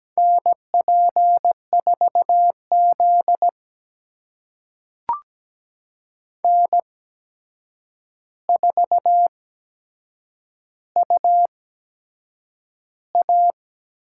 Morse code, heard as a keyed tone, about 17 wpm, spaced as for 7 wpm, 700 Hz. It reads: NP4Z N4UA
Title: USA Call Signs - 4 Characters - 8x Character Spacing - 17wpm